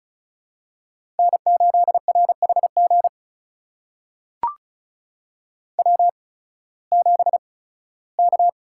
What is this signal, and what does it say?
Morse code, heard as a keyed tone, about 36 wpm, spaced as for 19 wpm, 700 Hz. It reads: N8RHG W7K
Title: USA Call Signs - 8x Character Spacing - 35wpm